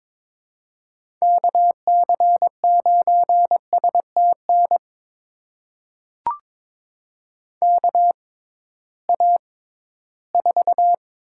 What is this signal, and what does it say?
Morse code, heard as a keyed tone, about 22 wpm, spaced as for 14 wpm, 700 Hz. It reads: KC9STN KA4
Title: USA Call Signs - 6x Character Spacing - 22wpm